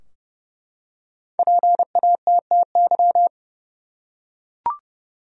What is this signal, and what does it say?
Morse code, heard as a keyed tone, about 30 wpm, 700 Hz. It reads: PATTY
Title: USA Common Names - Max 5 Letters - 30wpm